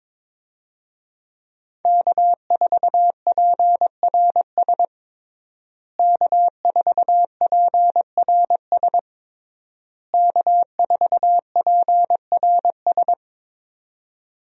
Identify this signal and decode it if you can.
Morse code, heard as a keyed tone, about 21 wpm, 700 Hz. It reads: K4PRS K4PRS K4PRS
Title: USA Call Signs - 5 Characters - Repeated 3x - 22wpm